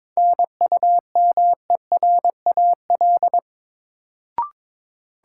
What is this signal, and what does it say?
Morse code, heard as a keyed tone, about 22 wpm, 700 Hz. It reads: NUMERAL
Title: Sentences from Top 1000 Words - Review Part 2 - New Word Drill - 22wpm